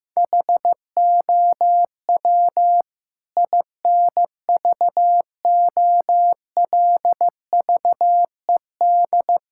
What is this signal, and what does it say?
Morse code, heard as a keyed tone, about 16 wpm, 700 Hz. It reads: HOW INVOLVED
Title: Sentences from Top 1000 Words - Review Part 5 - 15wpm